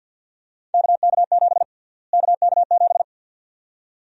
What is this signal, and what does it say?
Morse code, heard as a keyed tone, about 51 wpm, 700 Hz. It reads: KK7 KK7
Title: USA Call Sign Prefixes - Repeated 2x - 50wpm